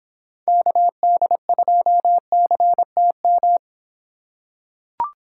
Morse code, heard as a keyed tone, about 25 wpm, 700 Hz. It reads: KD2CTM